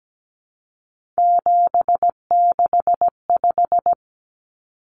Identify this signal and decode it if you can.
Morse code, heard as a keyed tone, about 17 wpm, 700 Hz. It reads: 765
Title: Numbers - 3-Digit 17wpm